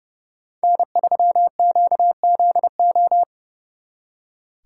Morse code, heard as a keyed tone, about 36 wpm, 700 Hz. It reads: N3QZO
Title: Speed Racing - USA 1x3 Call Signs 30wpm